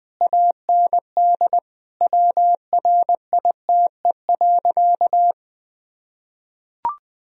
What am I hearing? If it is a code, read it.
Morse code, heard as a keyed tone, about 20 wpm, 700 Hz. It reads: AND WRITE.
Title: Sentences from Top 1000 Words - Review Part 1 - 20wpm